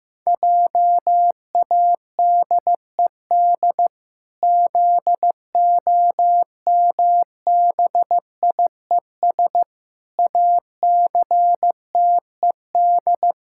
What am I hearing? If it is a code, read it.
Morse code, heard as a keyed tone, about 14 wpm, 700 Hz. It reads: JADED ZOMBIES ACTED
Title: Pangram Challenge v2 - 15wpm